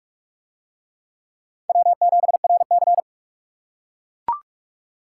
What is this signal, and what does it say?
Morse code, heard as a keyed tone, about 45 wpm, 700 Hz. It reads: W7RC